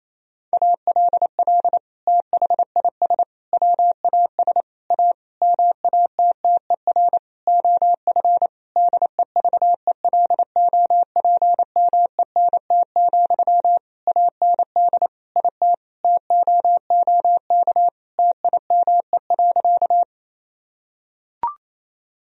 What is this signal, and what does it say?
Morse code, heard as a keyed tone, about 28 wpm, 700 Hz. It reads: ALL THIS WAS A MATTER OF DEVELOPMENT, AND IT TOOK TIME.
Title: Sentences from Top 1000 Words - Review Part 4 - 28wpm